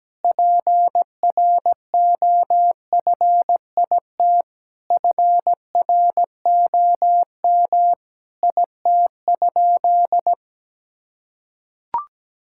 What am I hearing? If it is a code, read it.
Morse code, heard as a keyed tone, about 17 wpm, 700 Hz. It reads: PROFIT FROM IT?
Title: Sentences from Top 1000 Words - Review Part 5 - 17wpm